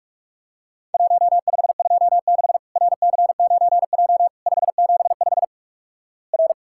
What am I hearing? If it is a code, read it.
Morse code, heard as a keyed tone, about 46 wpm, 700 Hz. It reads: 1526 RC9J 575 R
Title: Worked All Europe DX Contest - QTC Exchange - 45wpm